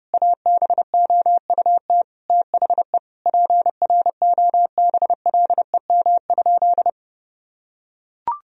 Morse code, heard as a keyed tone, about 30 wpm, 700 Hz. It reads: ABOUT THE PROBLEM?
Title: Sentences from Top 1000 Words - Review Part 4 - 30wpm